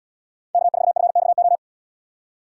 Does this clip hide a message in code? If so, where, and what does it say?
Yes: Morse code from 0.6 to 1.6 s, heard as a keyed tone, about 78 wpm, 700 Hz. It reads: K4FXO